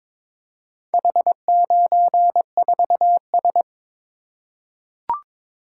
Morse code, heard as a keyed tone, about 22 wpm, 700 Hz. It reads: H94S
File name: practice / 22wpm / Four Letter-Number Combinations - 22wpm